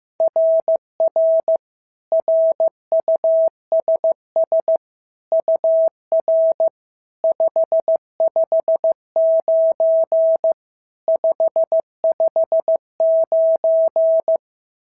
Morse code, heard as a keyed tone, about 14 wpm, 650 Hz. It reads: RR RUSS UR 559 559
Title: SOTA QSO Practice 15wpm